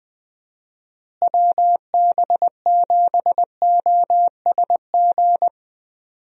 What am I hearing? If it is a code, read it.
Morse code, heard as a keyed tone, about 21 wpm, 700 Hz. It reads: WB7OSG
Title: USA Call Signs - 6 Characters - Repeated 3x - 20wpm